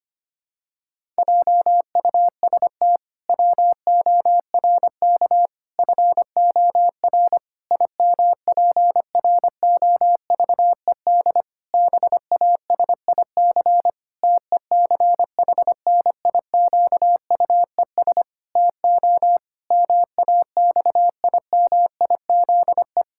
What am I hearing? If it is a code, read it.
Morse code, heard as a keyed tone, about 25 wpm, 700 Hz. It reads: JUST WORK FOR IMPROVED BASIC TECHNIQUES TO MAXIMIZE
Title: Pangram Challenge v2 - 25wpm